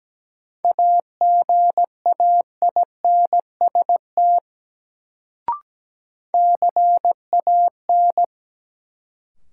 Morse code, heard as a keyed tone, about 17 wpm, 700 Hz. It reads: AGAINST CAN
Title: Sets of 2 Words - New Word Drill - 17wpm